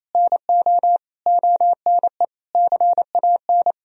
Morse code, heard as a keyed tone, about 28 wpm, 700 Hz. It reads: NO ONE CAN
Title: Sentences from Top 1000 Words - Review Part 2 - 28wpm